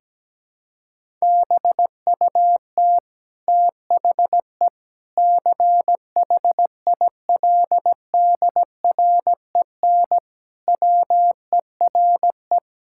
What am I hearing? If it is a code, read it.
Morse code, heard as a keyed tone, about 17 wpm, 700 Hz. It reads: BUT THE CHILDREN WERE
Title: Sentences from Top 1000 Words - Review Part 6 - 17wpm